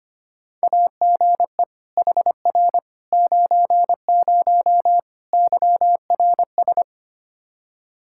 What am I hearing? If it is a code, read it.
Morse code, heard as a keyed tone, about 25 wpm, 700 Hz. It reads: AGE HR 90 YRS
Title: Practice QSO - Exchange 3 - More Info - 25wpm